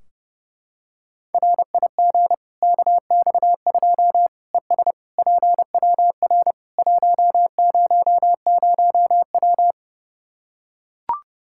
Morse code, heard as a keyed tone, about 32 wpm, 700 Hz. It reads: RIG KX2 ES PWR 100W
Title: K1USN SST QSO Practice 30wpm